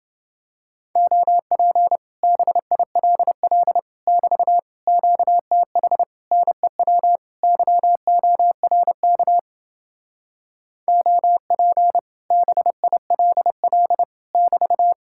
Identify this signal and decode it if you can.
Morse code, heard as a keyed tone, about 30 wpm, 700 Hz. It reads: OP BILL <BT> QTH NEW YORK OP BILL <BT>